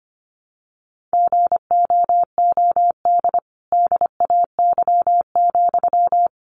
Morse code, heard as a keyed tone, about 25 wpm, 700 Hz. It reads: GOOD DAY,
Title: Sentences from Top 200 Words 25wpm